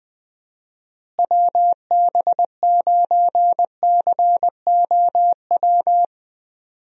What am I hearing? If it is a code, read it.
Morse code, heard as a keyed tone, about 20 wpm, 700 Hz. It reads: WB9COW
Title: USA Call Signs - 6 Characters - Repeated 2x - 20wpm